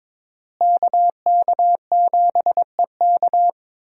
Morse code, heard as a keyed tone, about 22 wpm, 700 Hz. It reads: KK7EK
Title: USA Call Signs - 5 Characters - Repeated 2x - 22wpm